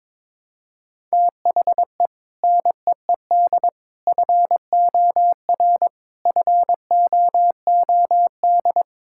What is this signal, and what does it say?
Morse code, heard as a keyed tone, about 22 wpm, 700 Hz. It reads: THE NEED FOR FOOD
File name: practice / 22wpm / Sentences with 3-Word Phrases - 22wpm